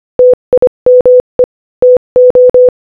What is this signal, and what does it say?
Morse code, heard as a keyed tone, about 25 wpm, 500 Hz. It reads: TIME TO
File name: Morse-Code-Audio-Speed-25-12